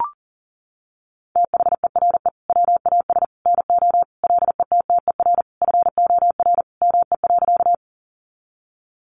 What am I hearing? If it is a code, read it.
Morse code, heard as a keyed tone, about 40 wpm, 700 Hz. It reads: THERE WAS NO LETTER FOR ME.
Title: Sentences from Top 300 Words 40wpm